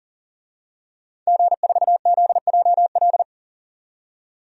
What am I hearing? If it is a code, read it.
Morse code, heard as a keyed tone, about 38 wpm, 700 Hz. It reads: G4ZJL
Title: DXCC-100 Call Signs - 5 Characters - 40wpm